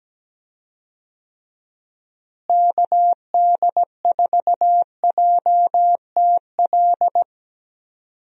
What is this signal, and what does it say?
Morse code, heard as a keyed tone, about 17 wpm, 700 Hz. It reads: KD4JTL
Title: USA Call Signs - 6 Characters - Repeated 2x - 17wpm